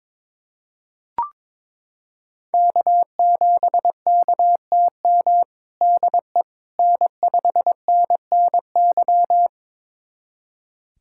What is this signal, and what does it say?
Morse code, heard as a keyed tone, about 22 wpm, 700 Hz. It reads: K7KTM DE N5NNY
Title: USA Callsign DE Callsign 22wpm